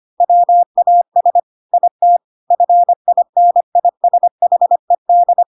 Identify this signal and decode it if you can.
Morse code, heard as a keyed tone, about 25 wpm, 700 Hz. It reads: WAS IT FINISHED
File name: Pinocchio / Pinocchio ch3 25 25 ik1hge